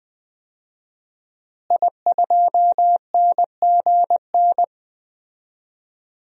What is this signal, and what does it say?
Morse code, heard as a keyed tone, about 20 wpm, 700 Hz. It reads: I2NGN
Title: DXCC-100 Call Signs - 5 Characters - Repeated 2x - 20wpm